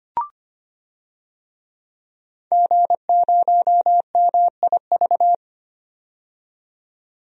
Morse code, heard as a keyed tone, about 25 wpm, 700 Hz. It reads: G0MIV
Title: DXCC-100 Call Signs - 5 Characters - 25wpm